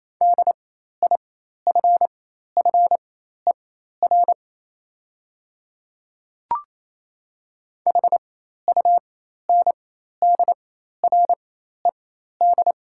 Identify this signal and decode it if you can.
Morse code, heard as a keyed tone, about 28 wpm, spaced as for 13 wpm, 700 Hz. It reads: DIFFER HUNDRED
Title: 4x Character Spacing - Top 500 Words - 28wpm